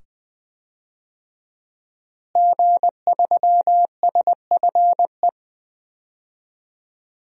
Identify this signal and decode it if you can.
Morse code, heard as a keyed tone, about 21 wpm, 700 Hz. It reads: G3SFE
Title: DXCC-100 Call Signs - 5 Characters - Repeated 2x - 20wpm